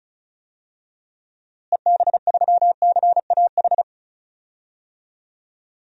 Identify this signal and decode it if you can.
Morse code, heard as a keyed tone, about 34 wpm, 700 Hz. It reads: EB3CAH
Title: DXCC-100 Call Signs - 6 Characters - Repeated 2x - 35wpm